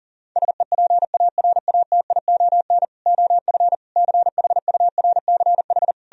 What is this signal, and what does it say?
Morse code, heard as a keyed tone, about 52 wpm, 700 Hz. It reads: SEPARATION OF CHURCH